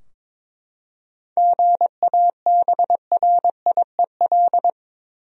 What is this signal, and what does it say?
Morse code, heard as a keyed tone, about 21 wpm, 700 Hz. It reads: GABRIEL
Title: USA Common Names - 22wpm